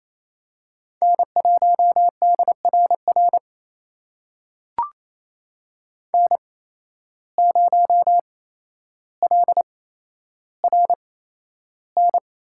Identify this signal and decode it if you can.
Morse code, heard as a keyed tone, about 28 wpm, spaced as for 12 wpm, 700 Hz. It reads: N1DRR N0LRN
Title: USA Call Signs - 5 Characters - 8x Character Spacing - 28wpm